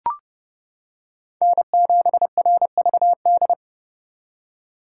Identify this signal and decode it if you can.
Morse code, heard as a keyed tone, about 30 wpm, 700 Hz. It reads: N7RVD
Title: K1USN SST QSO Practice - 30wpm